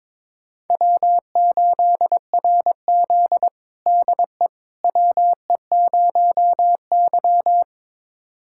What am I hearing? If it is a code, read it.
Morse code, heard as a keyed tone, about 22 wpm, 700 Hz. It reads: W8RZ DE WE0Y
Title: USA Callsign DE Callsign - 4 Characters - 22wpm